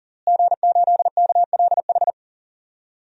Morse code, heard as a keyed tone, about 37 wpm, 700 Hz. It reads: G8KLH
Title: DXCC-100 Call Signs - 5 Characters - Repeated 3x - 40wpm